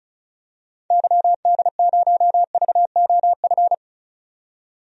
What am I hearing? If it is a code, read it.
Morse code, heard as a keyed tone, about 36 wpm, 700 Hz. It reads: YD0VOF
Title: DXCC-100 Call Signs - 6 Characters - 35wpm